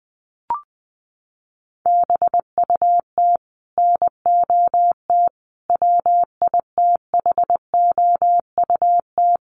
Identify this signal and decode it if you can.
Morse code, heard as a keyed tone, about 20 wpm, 700 Hz. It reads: BUT NOT WITHOUT
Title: Sentences from Top 1500 Words 20wpm